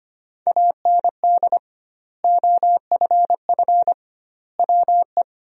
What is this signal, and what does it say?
Morse code, heard as a keyed tone, about 25 wpm, 700 Hz. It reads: AND OFF WE
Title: 2x Word-Spacing - Sentences from Top 200 Words - 25wpm